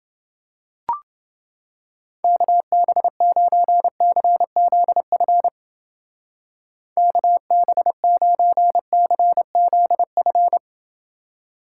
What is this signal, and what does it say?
Morse code, heard as a keyed tone, about 29 wpm, 700 Hz. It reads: KB9CZF KB9CZF